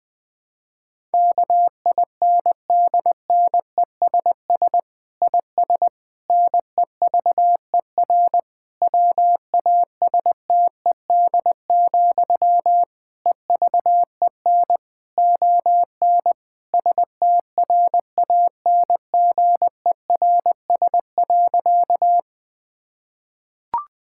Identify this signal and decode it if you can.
Morse code, heard as a keyed tone, about 21 wpm, 700 Hz. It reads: KINDNESS IS NEVER WASTED, EVEN ON STRANGERS.
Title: Difficult Sentences with Hints - 20wpm